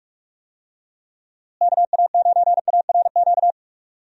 Morse code, heard as a keyed tone, about 46 wpm, 700 Hz. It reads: KA9ARQ
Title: USA Call Signs - 6 Characters - Repeated 2x - 45wpm